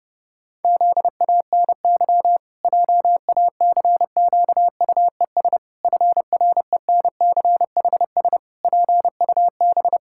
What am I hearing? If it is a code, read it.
Morse code, heard as a keyed tone, about 30 wpm, 700 Hz. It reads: ZANY JACQUES FRENCHS PUB
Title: Pangram Challenge v2 - 30wpm